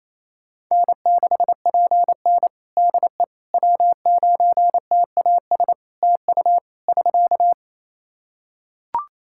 Morse code, heard as a keyed tone, about 28 wpm, 700 Hz. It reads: N6PN DE W9TAS TU <SK>
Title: Practice QSO - Exchange 4 - Ending - 28wpm